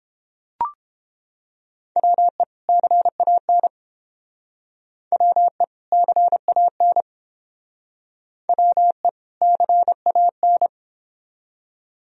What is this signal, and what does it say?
Morse code, heard as a keyed tone, about 30 wpm, 700 Hz. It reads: WE CAN WE CAN WE CAN